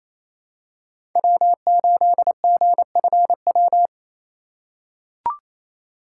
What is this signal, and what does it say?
Morse code, heard as a keyed tone, about 28 wpm, 700 Hz. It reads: W8GFW
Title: USA Call Signs - 5 Characters - 6x Character Spacing - 28wpm